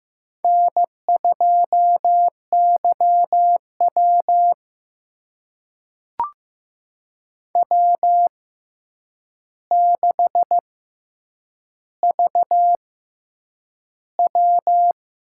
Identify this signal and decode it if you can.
Morse code, heard as a keyed tone, about 15 wpm, spaced as for 8 wpm, 700 Hz. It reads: N2YW W6VW
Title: USA Call Signs - 4 Characters - 6x Character Spacing - 15wpm